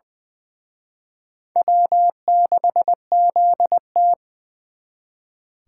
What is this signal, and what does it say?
Morse code, heard as a keyed tone, about 23 wpm, 700 Hz. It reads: W6ZT
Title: Speed Racing - USA 1x2 Call Signs 20wpm